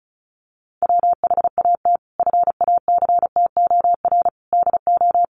Morse code, heard as a keyed tone, about 34 wpm, 700 Hz. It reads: WHAT FACTOR DO
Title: Sentences from Top 1300 Words 35wpm